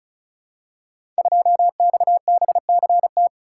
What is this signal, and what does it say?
Morse code, heard as a keyed tone, about 36 wpm, 700 Hz. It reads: 2XBCT
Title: Code Groups - 5 Letter:Numbers - 35wpm